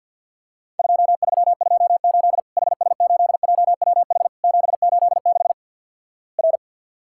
Morse code, heard as a keyed tone, about 52 wpm, 700 Hz. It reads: 2328 HS8JWH 786 R
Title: Worked All Europe DX Contest - QTC Exchange with Repeats - 50wpm